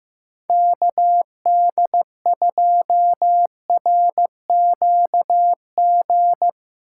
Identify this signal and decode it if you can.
Morse code, heard as a keyed tone, about 17 wpm, 700 Hz. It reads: KD2RQG
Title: Speed Racing - USA 2x3 Call Signs 15wpm